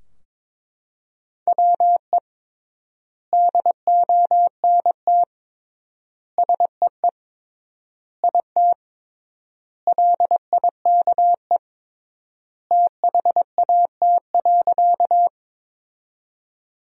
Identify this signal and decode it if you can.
Morse code, heard as a keyed tone, about 22 wpm, 700 Hz. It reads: WE DONT SEE IT LIKE THAT.
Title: 3x Word-Spacing - Sentences from Top 300 Words - 22wpm